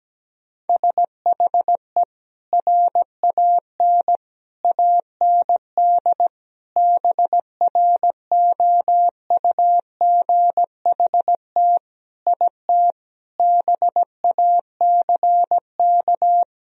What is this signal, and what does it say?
Morse code, heard as a keyed tone, about 17 wpm, 700 Hz. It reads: SHE RAN AND BROUGHT IT BACK
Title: Sentences from Top 1000 Words - Review Part 3 - 17wpm